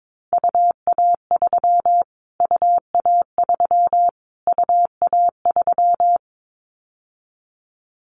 Morse code, heard as a keyed tone, about 22 wpm, 700 Hz. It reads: UA3 UA3 UA3
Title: Callsign Prefixes for Top 25 DXCC - Repeated 3x - 22wpm